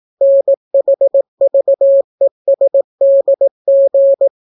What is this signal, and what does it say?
Morse code, heard as a keyed tone, about 18 wpm, 550 Hz. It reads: NHVESDG